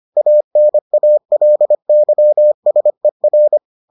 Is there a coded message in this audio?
Yes: Morse code from 0.2 to 3.6 s, heard as a keyed tone, about 25 wpm, 600 Hz. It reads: ANALYSER